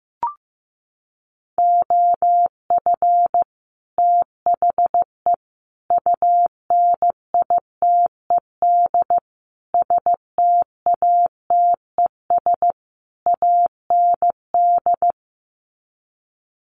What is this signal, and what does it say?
Morse code, heard as a keyed tone, about 15 wpm, 700 Hz. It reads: OF THE UNITED STATES AND
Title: Sets of 5 Words 15wpm